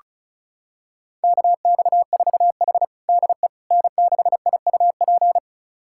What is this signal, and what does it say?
Morse code, heard as a keyed tone, about 36 wpm, 700 Hz. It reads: KX4H DE N6IUP
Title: USA Callsign DE Callsign 35wpm